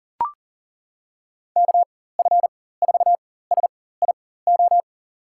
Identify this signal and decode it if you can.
Morse code, heard as a keyed tone, about 40 wpm, spaced as for 19 wpm, 700 Hz. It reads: KF4SIO
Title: USA Call Signs - 6 Characters - 4x Character Spacing - 40wpm